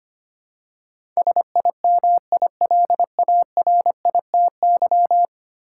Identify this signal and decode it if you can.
Morse code, heard as a keyed tone, about 25 wpm, 700 Hz. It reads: SIMILARITY
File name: Ten Letter Words - 25wpm